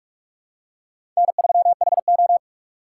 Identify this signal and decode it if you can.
Morse code, heard as a keyed tone, about 44 wpm, 700 Hz. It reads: N3HO
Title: USA Call Signs - 4 Characters - 2x Character Spacing - 45wpm